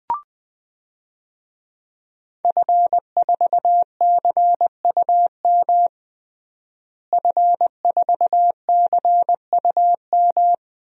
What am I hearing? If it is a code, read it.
Morse code, heard as a keyed tone, about 20 wpm, 700 Hz. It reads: F4CUM F4CUM
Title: DXCC-100 Call Signs - 5 Characters - Repeated 2x - 20wpm